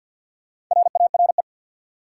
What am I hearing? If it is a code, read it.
Morse code, heard as a keyed tone, about 53 wpm, 700 Hz. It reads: RARE
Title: Four Letter Words - 50wpm